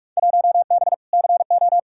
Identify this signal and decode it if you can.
Morse code, heard as a keyed tone, about 45 wpm, 700 Hz. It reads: 1B CO
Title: Field Day Exchanges - 45wpm